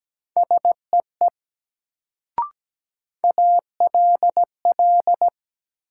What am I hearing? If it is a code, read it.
Morse code, heard as a keyed tone, about 17 wpm, 700 Hz. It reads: SEE ALL
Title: Sentences from Top 100 Words - New Word Drill - 17wpm